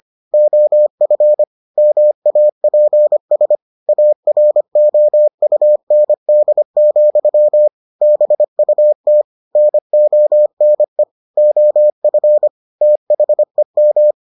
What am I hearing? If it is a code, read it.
Morse code, heard as a keyed tone, about 25 wpm, 600 Hz. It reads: OF MAPS AROUND, BUT NONE OF THEM